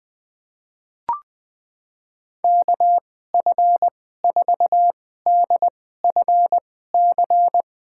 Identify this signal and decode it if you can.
Morse code, heard as a keyed tone, about 21 wpm, spaced as for 15 wpm, 700 Hz. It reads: KF4DFC
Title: USA Call Signs - 6 Characters - 2x Character Spacing - 20wpm